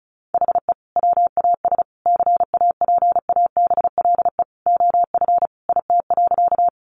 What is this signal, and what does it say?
Morse code, heard as a keyed tone, about 34 wpm, 700 Hz. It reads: HE WAS CAPABLE OF IT.
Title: Sentences from Top 1200 Words 35wpm